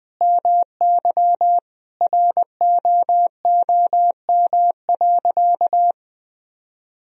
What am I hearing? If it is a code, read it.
Morse code, heard as a keyed tone, about 21 wpm, 700 Hz. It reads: MY ROOM.
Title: Sentences from Top 1000 Words - Review Part 1 - 20wpm